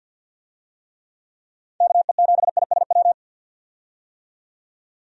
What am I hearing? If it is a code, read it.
Morse code, heard as a keyed tone, about 52 wpm, 700 Hz. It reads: KE7ISW
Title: USA Call Signs - 6 Characters - 50wpm